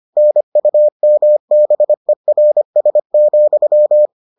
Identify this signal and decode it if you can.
Morse code, heard as a keyed tone, about 25 wpm, 600 Hz. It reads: NUMBERS,